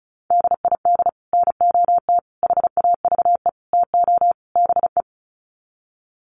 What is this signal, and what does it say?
Morse code, heard as a keyed tone, about 34 wpm, 700 Hz. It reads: DID NOT HAVE TO BE
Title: Sets of 5 Words Encore2 35wpm